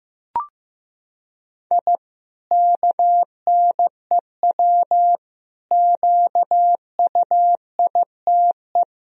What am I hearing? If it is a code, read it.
Morse code, heard as a keyed tone, about 16 wpm, 700 Hz. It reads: I KNEW QUITE
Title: Sentences from Top 1000 Words - Review Part 3 - 15wpm